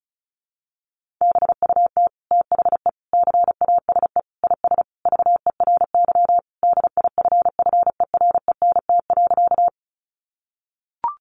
Morse code, heard as a keyed tone, about 34 wpm, 700 Hz. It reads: BUT THE CASE IS VERY DIFFERENT.
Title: Sentences from Top 700 Words 35wpm